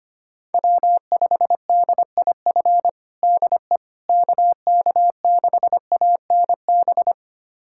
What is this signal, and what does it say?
Morse code, heard as a keyed tone, about 25 wpm, 700 Hz. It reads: W5DIF DE KK6ANB
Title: USA Callsign DE Callsign 25wpm